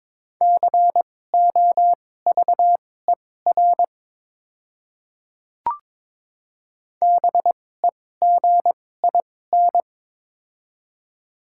Morse code, heard as a keyed tone, about 22 wpm, spaced as for 16 wpm, 700 Hz. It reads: COVER BEGIN
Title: 2x Character Spacing - Top 300 Words - 22wpm